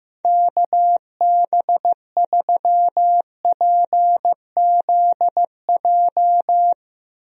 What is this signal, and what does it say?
Morse code, heard as a keyed tone, about 15 wpm, 700 Hz. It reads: KB3PZJ
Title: USA Call Signs - 6 Characters - Repeated 2x - 15wpm